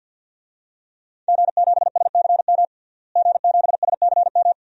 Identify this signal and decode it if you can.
Morse code, heard as a keyed tone, about 48 wpm, 700 Hz. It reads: G7SCM G7SCM
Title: DXCC-100 Call Signs - 5 Characters - Repeated 2x - 50wpm